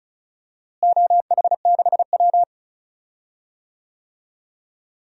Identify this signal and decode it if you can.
Morse code, heard as a keyed tone, about 36 wpm, 700 Hz. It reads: OH6W
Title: DXCC-100 Call Signs - 4 Characters - Repeated 2x - 35wpm